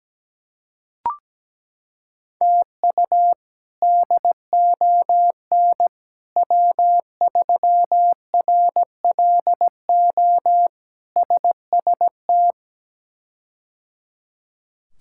Morse code, heard as a keyed tone, about 17 wpm, 700 Hz. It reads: TU DON W3RLO SST
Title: K1USN SST QSO Practice - 17wpm